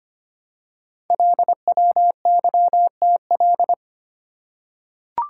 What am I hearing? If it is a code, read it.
Morse code, heard as a keyed tone, about 25 wpm, 700 Hz. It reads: LWYTL
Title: Code Groups - 5 Letters - 25wpm